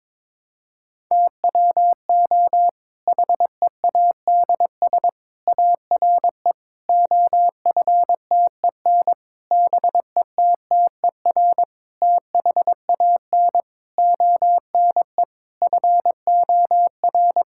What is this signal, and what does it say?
Morse code, heard as a keyed tone, about 22 wpm, 700 Hz. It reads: TWO HEADS ARE OFTEN BETTER THAN ONE FOR
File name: Difficult Sentences with Hints - 22wpm